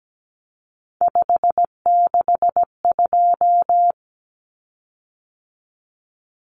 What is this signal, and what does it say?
Morse code, heard as a keyed tone, about 17 wpm, 700 Hz. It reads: 562
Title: Common Cut Numbers - 3-Digit 17wpm